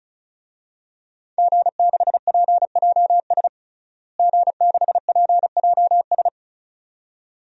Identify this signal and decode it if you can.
Morse code, heard as a keyed tone, about 34 wpm, 700 Hz. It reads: G6PJS G6PJS
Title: DXCC-100 Call Signs - 5 Characters - Repeated 2x - 35wpm